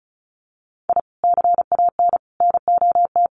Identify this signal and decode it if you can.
Morse code, heard as a keyed tone, about 34 wpm, 700 Hz. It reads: I CAN NOT
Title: Sets of 5 Words Encore2 35wpm